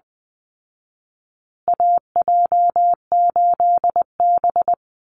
Morse code, heard as a keyed tone, about 20 wpm, 700 Hz. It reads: AJ8B
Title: Call Signs Repeated 2x - 20wpm